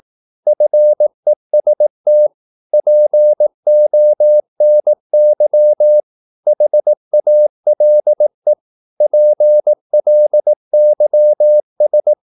Morse code, heard as a keyed tone, about 18 wpm, 600 Hz. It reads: FEST PONY HALE PLYS
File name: Korte ord DK 18wpm